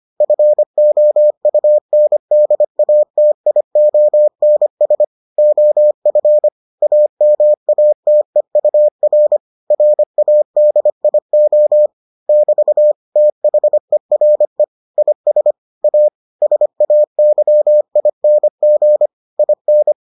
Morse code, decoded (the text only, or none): FOUNDATIONS OF AMATEUR RADIO <BT> THERE IS A SAYING IN